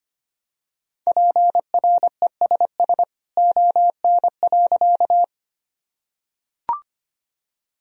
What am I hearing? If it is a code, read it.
Morse code, heard as a keyed tone, about 25 wpm, 700 Hz. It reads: PRESS ON.
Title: Sentences from Top 1000 Words - Review Part 1 - 25wpm